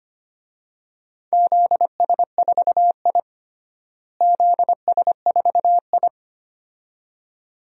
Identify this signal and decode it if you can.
Morse code, heard as a keyed tone, about 25 wpm, 700 Hz. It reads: ZS4I ZS4I
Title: DXCC-100 Call Signs - 4 Characters - Repeated 2x - 25wpm